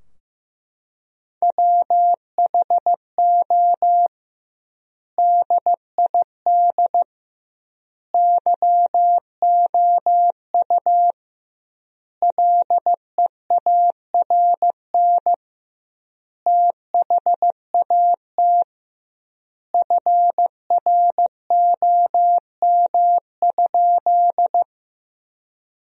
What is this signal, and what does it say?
Morse code, heard as a keyed tone, about 15 wpm, 700 Hz. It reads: WHO DID YOU LEARN THAT FROM?
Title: 2x Word-Spacing - Sentences from Top 300 Words - 15wpm